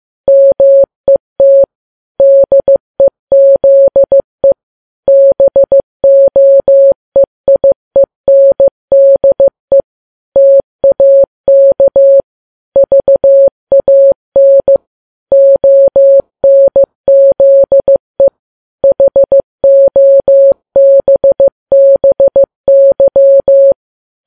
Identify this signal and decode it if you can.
Morse code, heard as a keyed tone, about 15 wpm, 550 Hz. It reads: MET DEZE BOEIENDE TAK VAN ONZE HOBBY
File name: Files Web tls / Morse on9fv / morse15kps